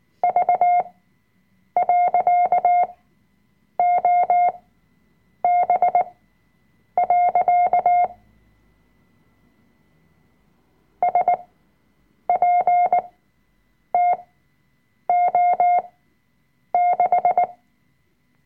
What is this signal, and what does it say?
Morse code, heard as a keyed tone, about 18 wpm, spaced as for 7 wpm, 710 Hz. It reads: V.OB. SPTO6
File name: Workshop Dateien / CW-Online-Kurs / Lektion 005